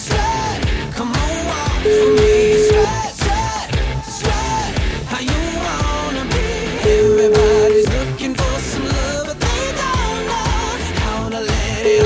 Ringback tone (Europe, 425 Hz), with noise.